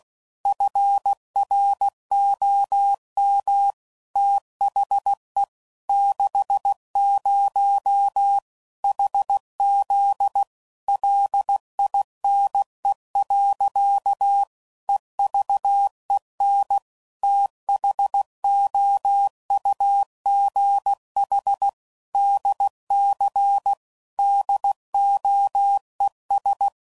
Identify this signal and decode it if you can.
Morse code, heard as a keyed tone, about 16 wpm, 800 Hz. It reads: FROM THE 60 HZ LINE. EVEN THOUGH DC DOES
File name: CW Code Groups @15WPM/Technical Subjects